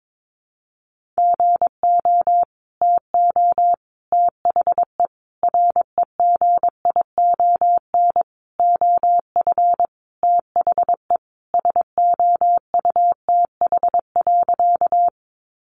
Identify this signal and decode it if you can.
Morse code, heard as a keyed tone, about 22 wpm, 700 Hz. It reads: GO TO THE REGION OF THE SOUTH.
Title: Sentences from Top 1500 Words 22wpm